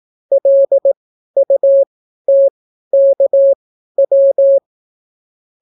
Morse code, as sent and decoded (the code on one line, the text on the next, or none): .-.. ..- - -.- .--
LUTKW